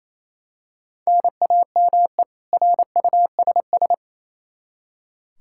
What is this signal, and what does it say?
Morse code, heard as a keyed tone, about 28 wpm, 700 Hz. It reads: NAME RUSS
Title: K1USN SST QSO Practice 28wpm